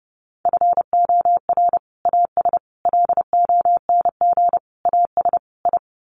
Morse code, heard as a keyed tone, about 30 wpm, 700 Hz. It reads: FOR AS LONG AS I
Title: Sets of 5 Words Encore2 30wpm